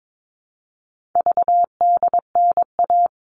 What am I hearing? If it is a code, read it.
Morse code, heard as a keyed tone, about 22 wpm, 700 Hz. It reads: VDNA
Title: All Cut Numbers - 4-Digit 22wpm